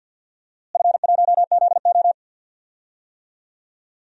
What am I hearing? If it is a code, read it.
Morse code, heard as a keyed tone, about 52 wpm, 700 Hz. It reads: F1ZO
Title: DXCC-100 Call Signs - 4 Characters - Repeated 2x - 50wpm